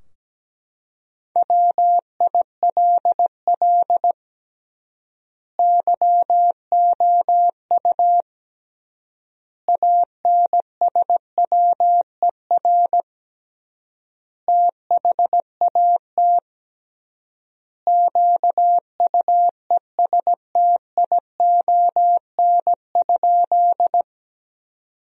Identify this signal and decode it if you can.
Morse code, heard as a keyed tone, about 17 wpm, 700 Hz. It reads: WILL YOU ANSWER THAT QUESTION?
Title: 3x Word-Spacing - Sentences from Top 400 Words - 17wpm